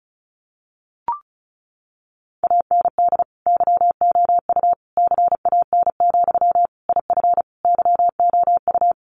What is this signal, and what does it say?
Morse code, heard as a keyed tone, about 34 wpm, 700 Hz. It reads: AND YOU CAN, IF YOU
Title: Sentences from Top 1500 Words 35wpm